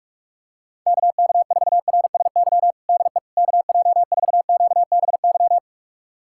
Morse code, heard as a keyed tone, about 41 wpm, 700 Hz. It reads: KK4RSY DE KJ4QBY
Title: USA Callsign DE Callsign - 6 Characters - 45wpm